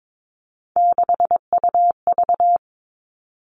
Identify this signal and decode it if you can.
Morse code, heard as a keyed tone, about 22 wpm, 700 Hz. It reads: 6UV
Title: All Cut Numbers - 3-Digit 22wpm